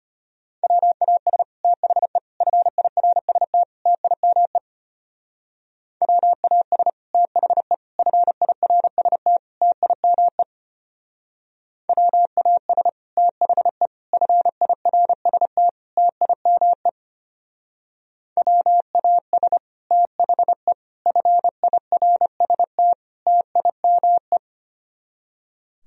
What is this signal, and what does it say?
Morse code, heard as a keyed tone, about 30 wpm, 700 Hz. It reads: WAS THE FIRST TIME WAS THE FIRST TIME WAS THE FIRST TIME WAS THE FIRST TIME
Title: Speed-Racing - Sets of 4 Words 25wpm